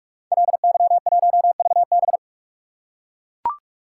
Morse code, heard as a keyed tone, about 44 wpm, 690 Hz. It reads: LY1VB
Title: Code Groups - 5 Letter:Numbers - 45wpm